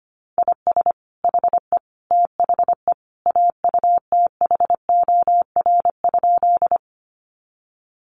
Morse code, heard as a keyed tone, about 25 wpm, 700 Hz. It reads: IS HE THE AUTHOR?
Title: Sentences from Top 1200 Words 25wpm